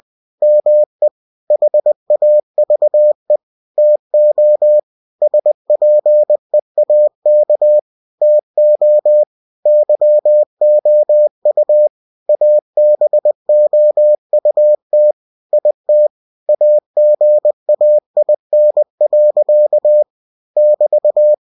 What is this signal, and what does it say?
Morse code, heard as a keyed tone, about 20 wpm, 600 Hz. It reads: ME HAVE TO SPEAK TO YOU ABOUT IT AGAIN. <BT>